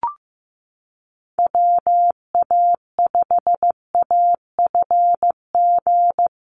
Morse code, heard as a keyed tone, about 15 wpm, 700 Hz. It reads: WA5AFG
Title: USA Call Signs Repeated 2x 15wpm